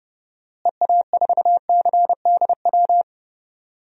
Morse code, heard as a keyed tone, about 29 wpm, 700 Hz. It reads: EA4CDW
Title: DXCC-100 Call Signs - 6 Characters - Repeated 3x - 30wpm